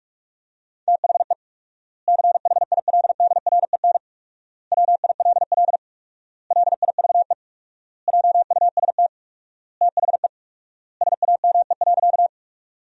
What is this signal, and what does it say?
Morse code, heard as a keyed tone, about 42 wpm, 700 Hz. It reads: THE CHILDREN WILL LIVE JUST THE SAME.
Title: 4x Word-Spacing - Sentences from Top 300 Words - 45wpm